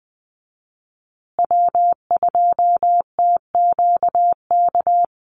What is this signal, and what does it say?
Morse code, heard as a keyed tone, about 20 wpm, 700 Hz. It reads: W2TQK
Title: USA Call Signs Repeated 3x 20wpm